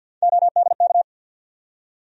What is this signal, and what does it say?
Morse code, heard as a keyed tone, about 52 wpm, 700 Hz. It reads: ODK
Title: USA Call Sign Postfixes - 3 Letters 50wpm